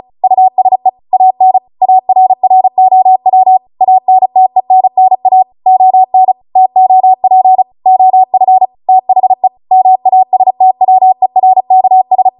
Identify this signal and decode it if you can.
Morse code, heard as a keyed tone, about 36 wpm, 750 Hz. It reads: USE AN ARROW ANTENNA ON TOP OF THE MASTWERKS